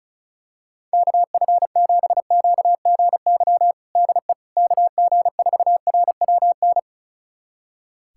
Morse code, heard as a keyed tone, about 36 wpm, 700 Hz. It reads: KF7QGY DE KG4RWN
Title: USA Callsign DE Callsign - 6 Characters - 35wpm